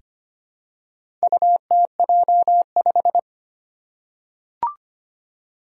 Morse code, heard as a keyed tone, about 25 wpm, 700 Hz. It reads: UTJ5
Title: Four Challenging Letter-Number Combinations 25wpm